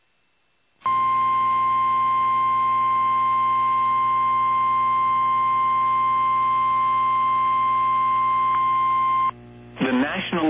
Emergency Alert System (SAME) tones (the 1050 Hz attention tone). It is heard from 0.8 to 9.3 s.